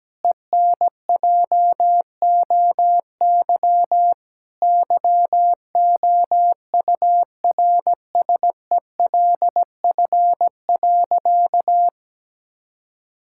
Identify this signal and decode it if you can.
Morse code, heard as a keyed tone, about 17 wpm, 700 Hz. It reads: ENJOY YOURSELF.
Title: Sentences from Top 1000 Words - Review Part 5 - 17wpm